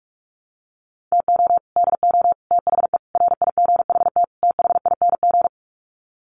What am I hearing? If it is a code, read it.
Morse code, heard as a keyed tone, about 44 wpm, 700 Hz. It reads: TO DO THE RIGHT THING
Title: Sets of 5 Words 45wpm